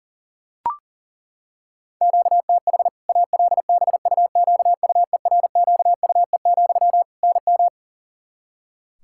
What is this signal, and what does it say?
Morse code, heard as a keyed tone, about 40 wpm, 700 Hz. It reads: QTH ALBUQUERQUE, NM
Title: Practice QSO - Exchange 1 - Beginning - 40wpm